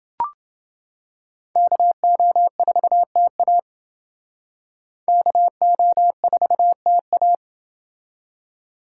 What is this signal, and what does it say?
Morse code, heard as a keyed tone, about 29 wpm, 700 Hz. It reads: KO4TA KO4TA